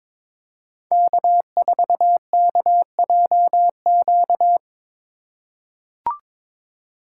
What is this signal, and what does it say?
Morse code, heard as a keyed tone, about 22 wpm, 700 Hz. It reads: K4KJQ
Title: USA Call Signs - 5 Characters - 3x Character Spacing - 22wpm